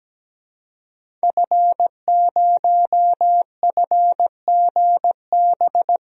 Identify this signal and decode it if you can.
Morse code, heard as a keyed tone, about 17 wpm, 700 Hz. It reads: F0FGB
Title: DXCC-100 Call Signs - 5 Characters - Repeated 3x - 17wpm